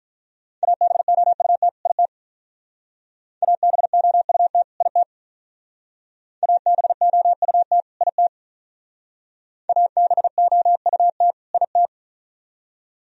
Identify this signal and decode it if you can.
Morse code, heard as a keyed tone, about 44 wpm, 690 Hz. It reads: ABOUT IT ABOUT IT ABOUT IT ABOUT IT